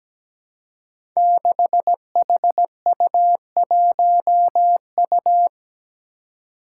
Morse code, heard as a keyed tone, about 17 wpm, 700 Hz. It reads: 6HU1U
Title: Five Challenging Letter-Number Combinations 17wpm